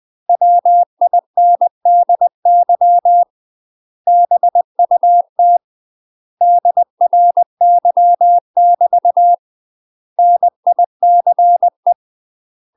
Morse code, heard as a keyed tone, about 20 wpm, 700 Hz. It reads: WINDY BUT DRY <BT> NICE
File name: UK MASTER QSOs TRAINING/20WPM